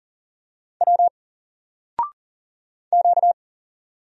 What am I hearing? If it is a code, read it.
Morse code, heard as a keyed tone, about 40 wpm, 700 Hz. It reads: W Q
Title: Single Letter-Number 40wpm